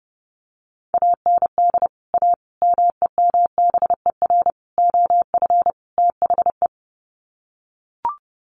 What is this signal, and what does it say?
Morse code, heard as a keyed tone, about 30 wpm, 700 Hz. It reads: AND A MEMBER OF THE
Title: Sets of 5 Words 30wpm